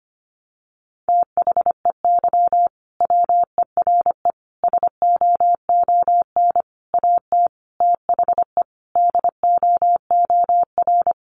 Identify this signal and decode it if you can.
Morse code, heard as a keyed tone, about 25 wpm, 700 Hz. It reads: THEY WERE SOON AT THE DOOR
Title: Sentences from Top 1200 Words 25wpm